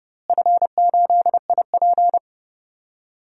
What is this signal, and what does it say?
Morse code, heard as a keyed tone, about 30 wpm, 700 Hz. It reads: F8IP
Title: DXCC-100 Call Signs - 4 Characters - Repeated 2x - 30wpm